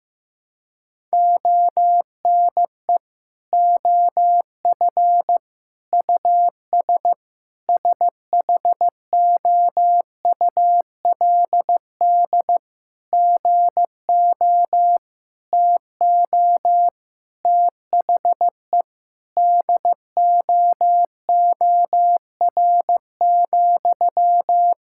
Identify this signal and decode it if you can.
Morse code, heard as a keyed tone, about 15 wpm, 700 Hz. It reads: ONE OF US SHOULD GO TO THE DOOR,
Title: Sentences from Top 1600 Words 15wpm